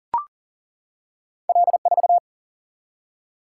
Morse code, heard as a keyed tone, about 40 wpm, 700 Hz. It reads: L4
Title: Single-Letter Single-Number 40wpm